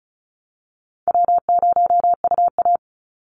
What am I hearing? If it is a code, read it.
Morse code, heard as a keyed tone, about 34 wpm, 700 Hz. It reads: W0UA
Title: Easy Call Signs Repeated 2x - 35wpm